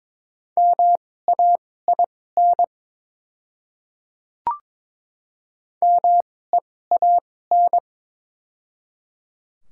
Morse code, heard as a keyed tone, about 22 wpm, spaced as for 16 wpm, 700 Hz. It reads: MAIN MEAN
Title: 2x Character Spacing - Top 300 Words - 22wpm